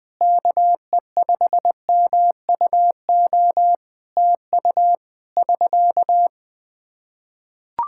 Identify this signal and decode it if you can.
Morse code, heard as a keyed tone, about 21 wpm, 700 Hz. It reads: KE5MUO TU <SK>
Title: K1USN SST QSO Practice 20wpm